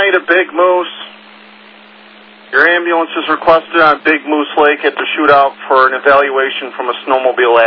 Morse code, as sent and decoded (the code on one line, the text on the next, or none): none